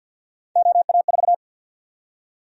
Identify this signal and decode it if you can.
Morse code, heard as a keyed tone, about 53 wpm, 700 Hz. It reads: OA4
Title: USA Call Sign Prefixes - Repeated 2x - 50wpm